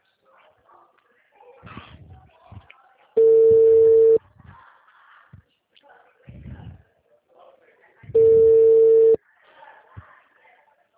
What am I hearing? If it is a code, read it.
Ringback tone (China, 450 Hz), with speech.